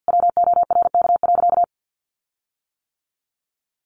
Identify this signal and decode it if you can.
Morse code, heard as a keyed tone, about 50 wpm, 700 Hz. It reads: WORK.
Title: Sentences from Top 1300 Words 50wpm